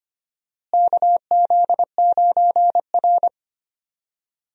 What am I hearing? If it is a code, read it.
Morse code, heard as a keyed tone, about 25 wpm, 700 Hz. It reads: KZ9R
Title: USA Call Signs - 4 Characters - Repeated 3x - 25wpm